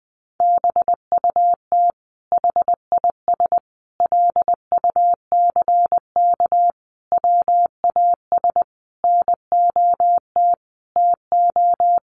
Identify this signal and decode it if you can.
Morse code, heard as a keyed tone, about 20 wpm, 700 Hz. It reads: BUT HIS LUCK WAS NOT TO
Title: Sentences from Top 1500 Words 20wpm